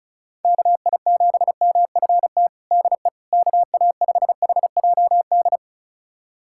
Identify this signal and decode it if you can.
Morse code, heard as a keyed tone, about 36 wpm, 700 Hz. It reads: KI7MFT DE KA5HJD
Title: USA Callsign DE Callsign - 6 Characters - 35wpm